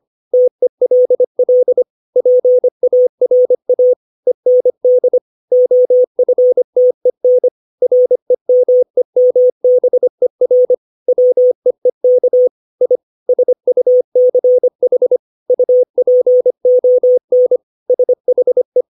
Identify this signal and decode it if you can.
Morse code, heard as a keyed tone, about 25 wpm, 500 Hz. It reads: TELL PARA END OFTEN REMEMBER WEEK I SUCH UPON SHE
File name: 632 words / 632 Words 25WPM